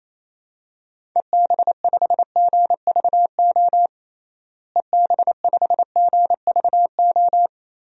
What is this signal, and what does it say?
Morse code, heard as a keyed tone, about 27 wpm, 700 Hz. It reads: EB5GVO EB5GVO
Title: DXCC-100 Call Signs - 6 Characters - Repeated 3x - 28wpm